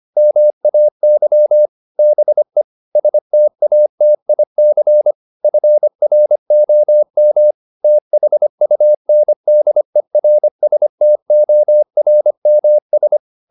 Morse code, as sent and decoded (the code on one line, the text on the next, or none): -- .- -.-- / -... / ... - .- - .. -.-. / ..-. .-. --- -- / - .... ..- -. -.. .-. ... - --- .-. -- ...
MAY BE STATIC FROM THUNDERSTORMS